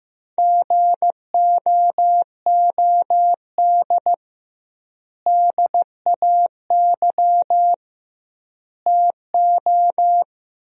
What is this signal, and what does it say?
Morse code, heard as a keyed tone, about 16 wpm, 700 Hz. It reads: GOOD DAY TO
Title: 2x Word-Spacing - Sentences from Top 200 Words - 15wpm